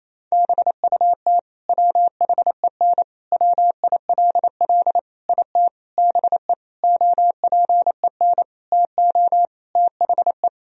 Morse code, heard as a keyed tone, about 28 wpm, 700 Hz. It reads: BUT WHEN WILL IT BE OPEN TO THE
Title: Sentences from Top 1000 Words - Review Part 3 - 28wpm